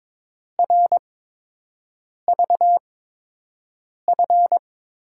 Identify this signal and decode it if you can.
Morse code, heard as a keyed tone, about 22 wpm, spaced as for 6 wpm, 700 Hz. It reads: RVF
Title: USA Call Signs - 6 Characters - 8x Character Spacing - 22wpm